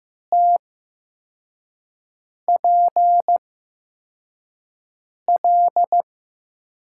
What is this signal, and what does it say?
Morse code, heard as a keyed tone, about 15 wpm, spaced as for 4 wpm, 700 Hz. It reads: TPL
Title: USA Call Signs - 5 Characters - 8x Character Spacing - 15wpm